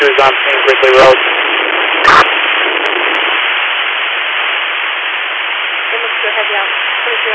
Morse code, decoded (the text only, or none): none